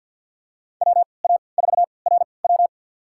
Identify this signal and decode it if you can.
Morse code, heard as a keyed tone, about 52 wpm, spaced as for 29 wpm, 700 Hz. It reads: WA4RW
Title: USA Call Signs - 5 Characters - 3x Character Spacing - 50wpm